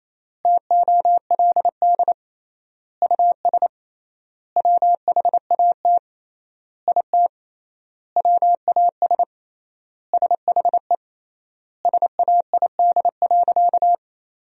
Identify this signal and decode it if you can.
Morse code, heard as a keyed tone, about 28 wpm, 700 Hz. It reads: TOLD US WHAT IT WAS SHE SAID.
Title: 3x Word-Spacing - Sentences from Top 400 Words - 28wpm